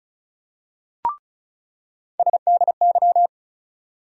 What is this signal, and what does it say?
Morse code, heard as a keyed tone, about 36 wpm, 700 Hz. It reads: SDY
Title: USA Call Sign Postfixes - 3 Letters 35wpm